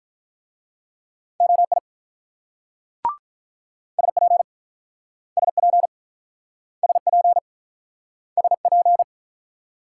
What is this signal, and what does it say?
Morse code, heard as a keyed tone, about 45 wpm, 700 Hz. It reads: OI SP SP SP SP